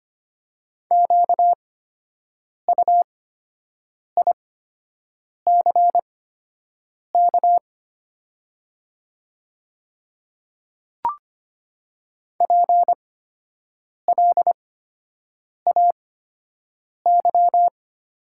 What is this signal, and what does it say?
Morse code, heard as a keyed tone, about 25 wpm, spaced as for 7 wpm, 700 Hz. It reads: QUICK PLAY
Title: 8x Character Spacing - Top 500 Words - 25wpm